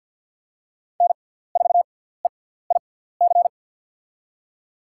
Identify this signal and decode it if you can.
Morse code, heard as a keyed tone, about 46 wpm, spaced as for 17 wpm, 700 Hz. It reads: N4EIC
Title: USA Call Signs - 5 Characters - 6x Character Spacing - 50wpm